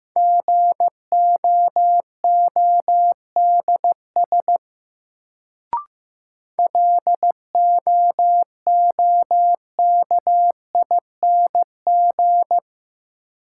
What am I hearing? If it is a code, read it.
Morse code, heard as a keyed tone, about 16 wpm, 700 Hz. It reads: GOODS LOOKING
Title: Sentences from Top 1000 Words - Review Part 6 - New Word Drill - 15wpm